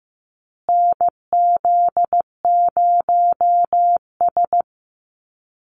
Morse code, heard as a keyed tone, about 15 wpm, 700 Hz. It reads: NZ0S
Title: USA Call Signs Repeated 3x 15wpm